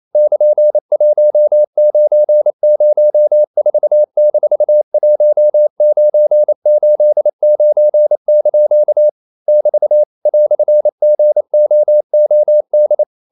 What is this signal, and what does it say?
Morse code, heard as a keyed tone, about 28 wpm, 600 Hz. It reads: <KN>1904-1989) <BT> "GOOD
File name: CW / Practice / EnglishQuotations0128